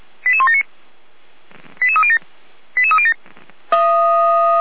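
Five-tone selective call (ZVEI), heard as quick runs of five tones, tones of about 70 ms. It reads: 88179 88307 88307
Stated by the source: ZVEI five-tone selective call 88171 received on SDR.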